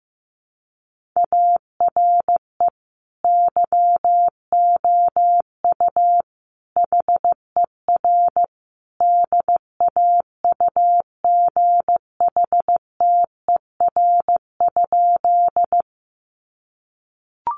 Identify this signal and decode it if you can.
Morse code, heard as a keyed tone, about 15 wpm, 700 Hz. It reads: ARE YOU HER DAUGHTER?
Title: Sentences from Top 1500 Words 15wpm